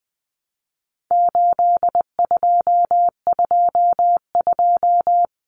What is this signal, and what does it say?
Morse code, heard as a keyed tone, about 20 wpm, 700 Hz. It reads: 8222
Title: Common Cut Numbers - 4-Digit 20wpm